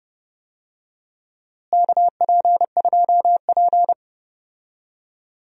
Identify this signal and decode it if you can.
Morse code, heard as a keyed tone, about 31 wpm, 700 Hz. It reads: KP2P
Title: DXCC-100 Call Signs - 4 Characters - Repeated 2x - 30wpm